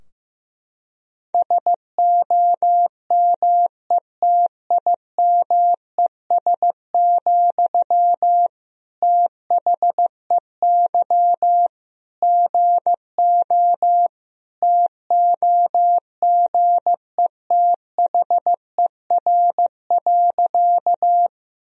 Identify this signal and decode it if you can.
Morse code, heard as a keyed tone, about 15 wpm, 700 Hz. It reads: SOMETIMES, THEY GO TOGETHER.
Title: Sentences from Top 1000 Words - Review Part 3 - 15wpm